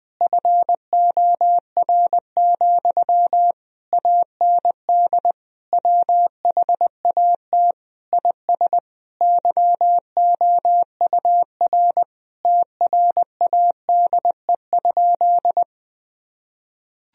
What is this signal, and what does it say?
Morse code, heard as a keyed tone, about 20 wpm, 700 Hz. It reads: FOR, AND WHAT IS YOUR TRADE?
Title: Sentences from Top 1000 Words - Review Part 4 - 20wpm